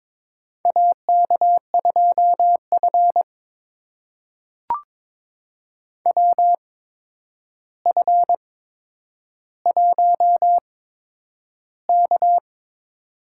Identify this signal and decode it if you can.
Morse code, heard as a keyed tone, about 22 wpm, spaced as for 9 wpm, 700 Hz. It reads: AK2F WF1K
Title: USA Call Signs - 4 Characters - 8x Character Spacing - 22wpm